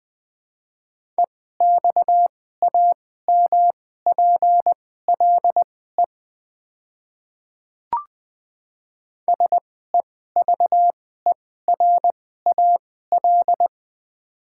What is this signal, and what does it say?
Morse code, heard as a keyed tone, about 20 wpm, spaced as for 14 wpm, 700 Hz. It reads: EXAMPLE SEVERAL
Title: 2x Character Spacing - Top 500 Words - 20wpm